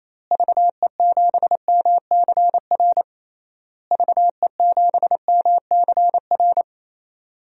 Morse code, heard as a keyed tone, about 28 wpm, 700 Hz. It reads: VE7MCR VE7MCR
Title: DXCC-100 Call Signs - 6 Characters - Repeated 3x - 28wpm